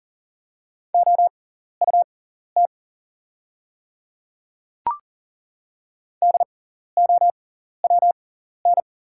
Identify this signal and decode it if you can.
Morse code, heard as a keyed tone, about 40 wpm, spaced as for 14 wpm, 700 Hz. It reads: OUT DOWN
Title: 6x Character Spacing - Top 100 Words - 40wpm